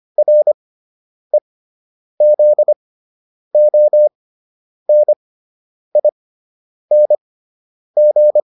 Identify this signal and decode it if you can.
Morse code, heard as a keyed tone, about 25 wpm, spaced as for 9 wpm, 600 Hz. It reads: REZONING